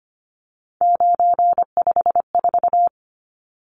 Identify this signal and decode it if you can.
Morse code, heard as a keyed tone, about 25 wpm, 700 Hz. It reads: 954
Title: Numbers - 3-Digit 25wpm